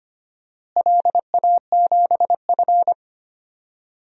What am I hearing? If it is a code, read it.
Morse code, heard as a keyed tone, about 25 wpm, 700 Hz. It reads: LA7F
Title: DXCC-100 Call Signs - 4 Characters - Repeated 2x - 25wpm